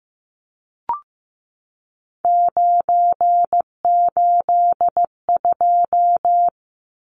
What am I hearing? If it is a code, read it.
Morse code, heard as a keyed tone, about 15 wpm, 700 Hz. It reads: 982
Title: Numbers - 3-Digit 15wpm